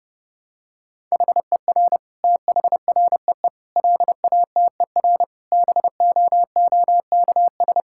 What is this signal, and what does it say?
Morse code, heard as a keyed tone, about 32 wpm, 700 Hz. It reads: HER THREE LATER BOOKS
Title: Sentences from Top 1000 Words - Review Part 6 - 30wpm